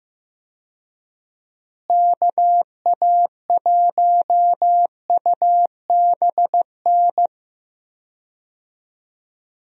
Morse code, heard as a keyed tone, about 15 wpm, 700 Hz. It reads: KA1UBN
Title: USA Call Signs - 6 Characters - Repeated 2x - 15wpm